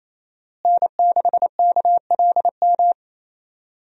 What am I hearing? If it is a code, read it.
Morse code, heard as a keyed tone, about 28 wpm, 700 Hz. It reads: N6KLM
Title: USA Call Signs - 5 Characters - Repeated 2x - 28wpm